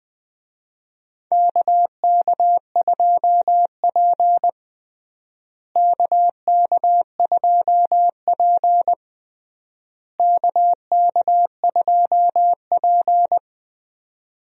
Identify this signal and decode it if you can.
Morse code, heard as a keyed tone, about 19 wpm, 700 Hz. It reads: KK2P KK2P KK2P
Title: USA Call Signs - 4 Characters - Repeated 3x - 20wpm